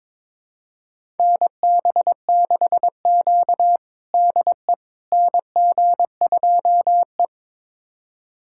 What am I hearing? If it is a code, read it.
Morse code, heard as a keyed tone, about 22 wpm, 700 Hz. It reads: NB6Q DE NG2E
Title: USA Callsign DE Callsign - 4 Characters - 22wpm